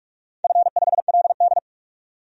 Morse code, heard as a keyed tone, about 46 wpm, 700 Hz. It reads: F5LD
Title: DXCC-100 Call Signs - 4 Characters - 45wpm